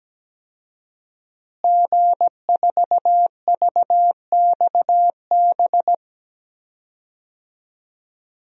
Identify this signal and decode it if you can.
Morse code, heard as a keyed tone, about 17 wpm, 700 Hz. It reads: G4VXB
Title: DXCC-100 Call Signs - 5 Characters - Repeated 2x - 17wpm